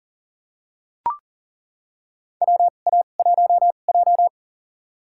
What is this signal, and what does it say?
Morse code, heard as a keyed tone, about 40 wpm, spaced as for 29 wpm, 700 Hz. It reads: WA1J
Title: USA Call Signs - 4 Characters - 2x Character Spacing - 40wpm